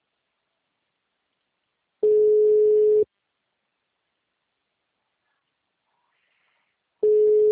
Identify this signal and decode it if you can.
Ringback tone (Europe, 425 Hz).